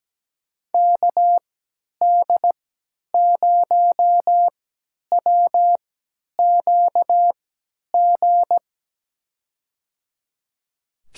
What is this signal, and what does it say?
Morse code, heard as a keyed tone, about 17 wpm, spaced as for 10 wpm, 700 Hz. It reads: KD0WQG
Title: USA Call Signs - 6 Characters - 3x Character Spacing - 17wpm